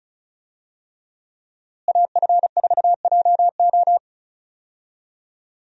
Morse code, heard as a keyed tone, about 34 wpm, 700 Hz. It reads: AF4JO
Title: USA Call Signs - 5 Characters - Repeated 3x - 35wpm